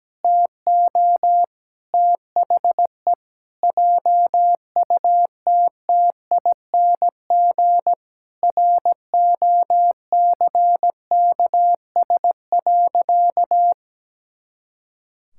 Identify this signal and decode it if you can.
Morse code, heard as a keyed tone, about 17 wpm, 700 Hz. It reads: TO THE JUTTING ROCKS.
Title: Pangram Challenge v2 - 17wpm